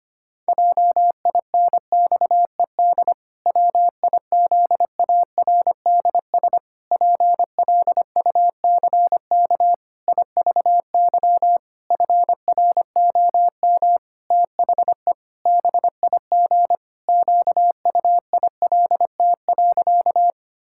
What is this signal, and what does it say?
Morse code, heard as a keyed tone, about 25 wpm, 700 Hz. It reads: JINXED WIZARDS PLUCK IVY FROM THE BIG QUILT.
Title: Pangram Challenge v2 - 25wpm